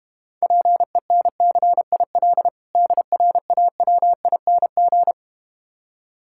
Morse code, heard as a keyed tone, about 32 wpm, 690 Hz. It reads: PENCIL DRAWING